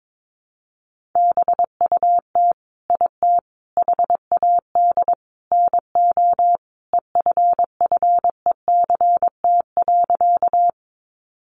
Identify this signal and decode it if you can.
Morse code, heard as a keyed tone, about 22 wpm, 700 Hz. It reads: BUT IT HAD NO EFFECT.
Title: Sentences from Top 1000 Words 22wpm